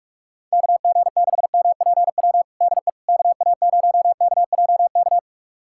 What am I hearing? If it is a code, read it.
Morse code, heard as a keyed tone, about 42 wpm, 700 Hz. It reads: KG6MPW DE KA0KJK
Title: USA Callsign DE Callsign - 6 Characters - 45wpm